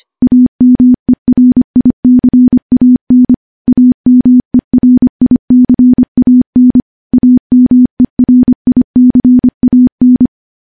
Morse code, heard as a keyed tone, about 25 wpm, 250 Hz. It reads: AMERICAN AMERICAN AMERICAN
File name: morse-training-25WPM